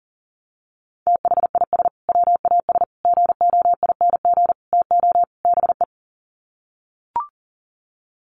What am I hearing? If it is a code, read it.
Morse code, heard as a keyed tone, about 40 wpm, 700 Hz. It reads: THIS WAS GOING TO BE
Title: Sets of 5 Words Encore2 40wpm